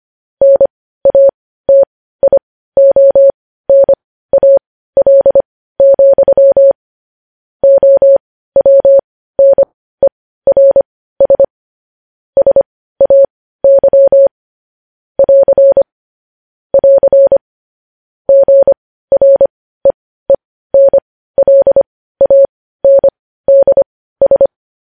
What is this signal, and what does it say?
Morse code, heard as a keyed tone, about 25 wpm, spaced as for 15 wpm, 550 Hz. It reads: NATIONAL, OWNERS SAY <AR> <AR> GREENLANDS
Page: news-headlines-in-morse-code-at-15-wpm